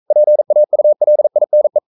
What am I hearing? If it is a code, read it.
Morse code, heard as a keyed tone, about 42 wpm, 600 Hz. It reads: PAULINE
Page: Romanen Never af Ken Follet på engelsk.